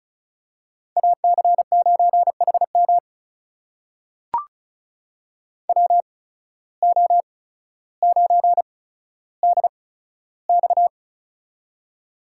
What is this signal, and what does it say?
Morse code, heard as a keyed tone, about 36 wpm, spaced as for 15 wpm, 700 Hz. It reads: AC9HM WO9DX
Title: USA Call Signs - 5 Characters - 8x Character Spacing - 35wpm